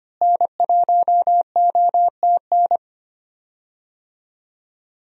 Morse code, heard as a keyed tone, about 25 wpm, 700 Hz. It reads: N1OTN
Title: USA Call Signs - 5 Characters - Repeated 2x - 25wpm